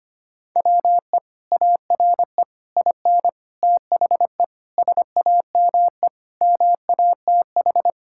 Morse code, heard as a keyed tone, about 25 wpm, 700 Hz. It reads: WE ARE IN THE SAME MATH
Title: Sentences from Top 1000 Words - Review Part 6 - 25wpm